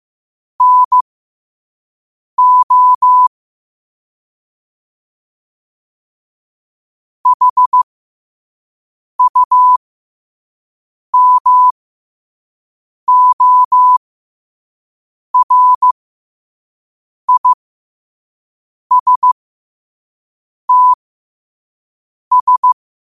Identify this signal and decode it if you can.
Morse code, heard as a keyed tone, about 15 wpm, spaced as for 5 wpm, 1000 Hz. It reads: NO HUMORISTS
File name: qotdfiles / 5wpm